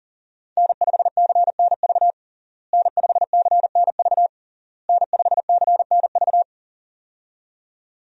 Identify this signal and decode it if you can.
Morse code, heard as a keyed tone, about 40 wpm, 700 Hz. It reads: N5CNV N5CNV N5CNV
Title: USA Call Signs - 5 Characters - Repeated 3x - 40wpm